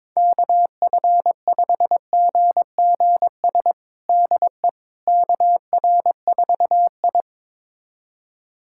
Morse code, heard as a keyed tone, about 22 wpm, 700 Hz. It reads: KF5GGS DE KR4I
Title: USA Callsign DE Callsign 22wpm